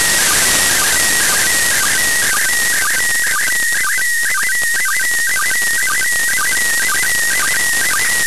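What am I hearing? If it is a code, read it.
SSTV (slow-scan television), PD 120.
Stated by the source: SSTV images sent from the ISS in memory of NASA astronaut Owen Garriott.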